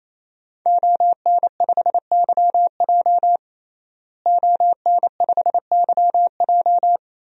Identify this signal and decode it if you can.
Morse code, heard as a keyed tone, about 28 wpm, 700 Hz. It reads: ON5YJ ON5YJ
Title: DXCC-100 Call Signs - 5 Characters - Repeated 3x - 28wpm